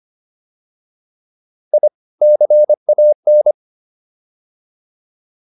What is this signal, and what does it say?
Morse code, heard as a keyed tone, about 25 wpm, 600 Hz. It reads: I CAN